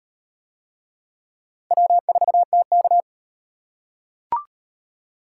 Morse code, heard as a keyed tone, about 36 wpm, 700 Hz. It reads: W4TK